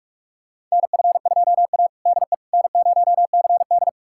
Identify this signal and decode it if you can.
Morse code, heard as a keyed tone, about 46 wpm, 690 Hz. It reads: NF2A DE N0CD
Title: USA Callsign DE Callsign - 4 Characters - 45wpm